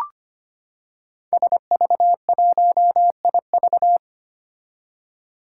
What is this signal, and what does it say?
Morse code, heard as a keyed tone, about 25 wpm, 700 Hz. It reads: SV1IV
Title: Five Challenging Letter-Number Combinations 25wpm